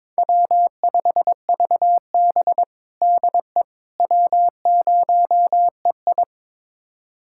Morse code, heard as a keyed tone, about 22 wpm, 700 Hz. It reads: W5VB DE W0EI
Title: USA Callsign DE Callsign - 4 Characters - 22wpm